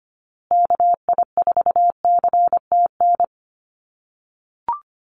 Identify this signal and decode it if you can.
Morse code, heard as a keyed tone, about 25 wpm, 700 Hz. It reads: KI4CTN
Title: USA Call Signs Repeated 2x 25wpm